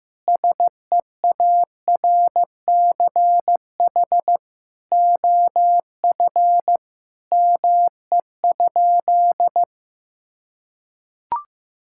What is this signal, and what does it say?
Morse code, heard as a keyed tone, about 15 wpm, 700 Hz. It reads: SEARCH OF ME?
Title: Sentences from Top 1000 Words - Review Part 5 - 15wpm